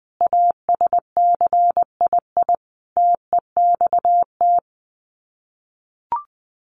Morse code, heard as a keyed tone, about 20 wpm, 700 Hz. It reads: ASCII TEXT
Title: Sets of 5 Words 20wpm